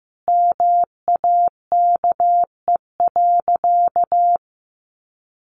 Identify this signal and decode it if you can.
Morse code, heard as a keyed tone, about 15 wpm, 700 Hz. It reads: MAKE.
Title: Sentences from Top 1100 Words 15wpm